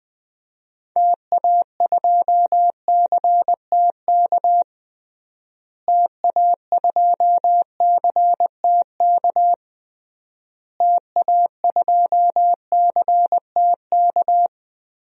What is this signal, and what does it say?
Morse code, heard as a keyed tone, about 20 wpm, 700 Hz. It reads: TA2CTK TA2CTK TA2CTK
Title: DXCC-100 Call Signs - 6 Characters - Repeated 3x - 20wpm